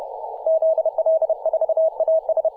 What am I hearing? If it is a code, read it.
Morse code, heard as a keyed tone, about 30 wpm, 640 Hz. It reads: ZL4AS
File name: posts / 2017-apr-02-0450z-3d2ag-p-18-087mhz-cw